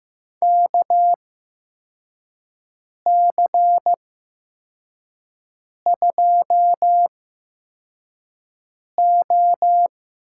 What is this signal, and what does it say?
Morse code, heard as a keyed tone, about 15 wpm, spaced as for 4 wpm, 700 Hz. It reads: KC2O
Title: USA Call Signs - 6 Characters - 8x Character Spacing - 15wpm